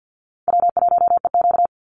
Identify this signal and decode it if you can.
Morse code, heard as a keyed tone, about 50 wpm, 700 Hz. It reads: W1EQ
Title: Call Signs - 50wpm